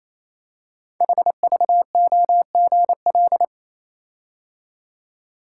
Morse code, heard as a keyed tone, about 27 wpm, 700 Hz. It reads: HVOGL
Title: Code Groups - 5 Letter:Numbers - 28wpm